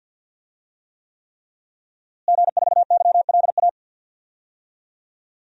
Morse code, heard as a keyed tone, about 52 wpm, 700 Hz. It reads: G4YLA
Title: DXCC-100 Call Signs - 5 Characters - Repeated 3x - 50wpm